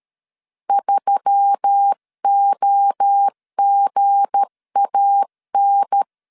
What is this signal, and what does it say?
Morse code, heard as a keyed tone, about 13 wpm, 800 Hz. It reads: 3OGAN
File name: LEZIONI / 7 GF9Z / 7 G 12wpm 2055